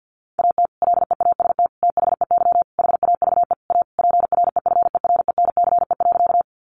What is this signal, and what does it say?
Morse code, heard as a keyed tone, about 50 wpm, 700 Hz. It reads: AT LEAST THEY HAVE A PREFERENCE.
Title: Sentences from Top 1500 Words 50wpm